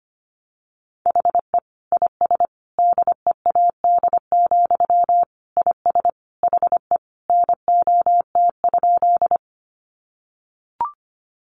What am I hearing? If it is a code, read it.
Morse code, heard as a keyed tone, about 25 wpm, 700 Hz. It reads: HE IS DEAD, IS HE NOT?
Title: Sentences from Top 1500 Words 25wpm